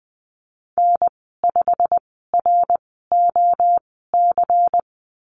Morse code, heard as a keyed tone, about 20 wpm, spaced as for 15 wpm, 700 Hz. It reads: N5ROC
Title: 245 Lesson 39g - Letter z - Call Signs 20wpm